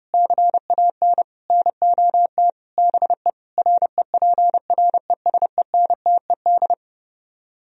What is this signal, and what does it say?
Morse code, heard as a keyed tone, about 30 wpm, 700 Hz. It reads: CAN NOT BE REPRESENTED